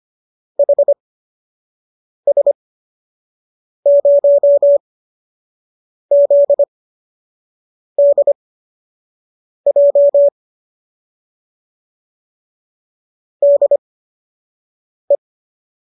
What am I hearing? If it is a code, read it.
Morse code, heard as a keyed tone, about 25 wpm, spaced as for 6 wpm, 570 Hz. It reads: HS0ZDJ DE